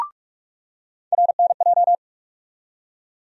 Morse code, heard as a keyed tone, about 43 wpm, 700 Hz. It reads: RNJ
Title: USA Call Sign Postfixes - 3 Letters 45wpm